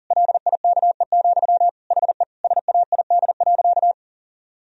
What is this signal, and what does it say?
Morse code, heard as a keyed tone, about 40 wpm, 700 Hz. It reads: LIKE, HE SAID.
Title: Sentences from Top 1000 Words - Review Part 2 - 40wpm